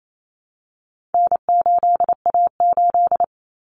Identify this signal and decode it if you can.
Morse code, heard as a keyed tone, about 28 wpm, 700 Hz. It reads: N8A8
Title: Common Cut Numbers - 4-Digit 28wpm